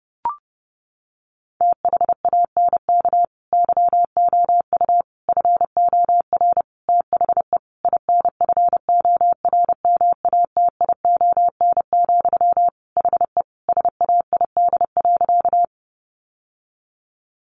Morse code, heard as a keyed tone, about 30 wpm, 700 Hz. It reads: THANK YOU FOR THE INFORMATION, HE SAID.
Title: Sentences from Top 600 Words 30wpm